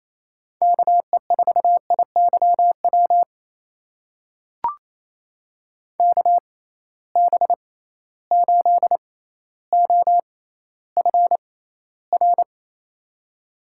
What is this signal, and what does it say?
Morse code, heard as a keyed tone, about 28 wpm, spaced as for 14 wpm, 700 Hz. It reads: KE4IYW KB8OFR
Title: USA Call Signs - 6 Characters - 6x Character Spacing - 28wpm